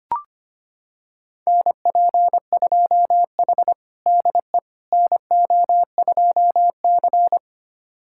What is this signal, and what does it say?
Morse code, heard as a keyed tone, about 25 wpm, 700 Hz. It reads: NP2H DE NO2C
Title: USA Callsign DE Callsign - 4 Characters - 25wpm